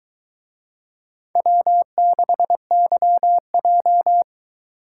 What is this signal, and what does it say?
Morse code, heard as a keyed tone, about 22 wpm, 700 Hz. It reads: W6YJ